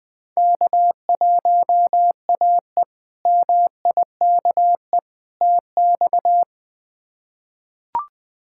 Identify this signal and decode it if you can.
Morse code, heard as a keyed tone, about 20 wpm, 700 Hz. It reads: K1AE MIKE TX
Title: K1USN SST QSO Practice - 20wpm